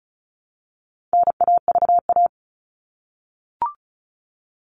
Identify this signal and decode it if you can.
Morse code, heard as a keyed tone, about 34 wpm, 700 Hz. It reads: NAVA
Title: All Cut Numbers - 4-Digit 35wpm